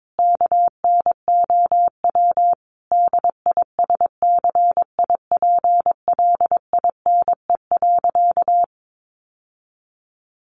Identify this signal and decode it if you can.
Morse code, heard as a keyed tone, about 22 wpm, 700 Hz. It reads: KNOW DISCIPLINE.
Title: Sentences from Top 900 Words 22wpm